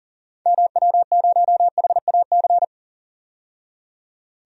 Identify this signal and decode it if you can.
Morse code, heard as a keyed tone, about 38 wpm, 700 Hz. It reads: MW0HAC
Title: DXCC-100 Call Signs - 6 Characters - 40wpm